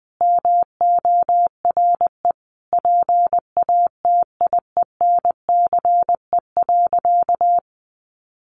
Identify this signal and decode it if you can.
Morse code, heard as a keyed tone, about 20 wpm, 700 Hz. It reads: MORE PATIENCE.
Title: Sentences from Top 1300 Words 20wpm